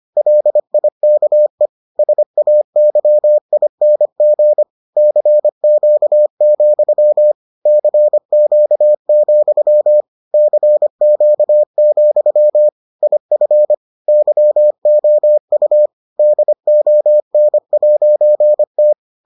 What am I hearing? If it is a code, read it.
Morse code, heard as a keyed tone, about 25 wpm, 600 Hz. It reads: LIKE SAYING CQ, CQ, CQ, IF YOU DON'T